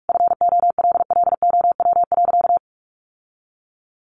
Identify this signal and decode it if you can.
Morse code, heard as a keyed tone, about 44 wpm, 700 Hz. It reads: FOLLOW.
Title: Sentences from Top 1500 Words 45wpm